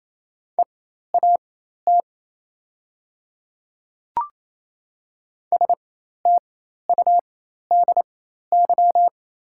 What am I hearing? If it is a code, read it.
Morse code, heard as a keyed tone, about 28 wpm, spaced as for 13 wpm, 700 Hz. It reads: EAT STUDY
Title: 4x Character Spacing - Top 300 Words - 28wpm